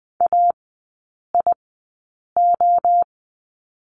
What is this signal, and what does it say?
Morse code, heard as a keyed tone, about 20 wpm, spaced as for 8 wpm, 700 Hz. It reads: AIO
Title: 041 Lesson 8e - Number 1 - Three Characters 20wpm